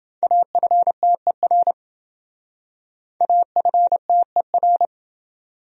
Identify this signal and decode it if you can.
Morse code, heard as a keyed tone, about 28 wpm, 700 Hz. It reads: AFTER AFTER